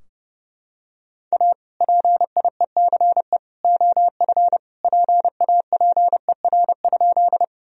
Morse code, heard as a keyed tone, about 30 wpm, 700 Hz. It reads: A PIECE OF PAPER?
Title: Sentences from Top 1000 Words - Review Part 2 - 30wpm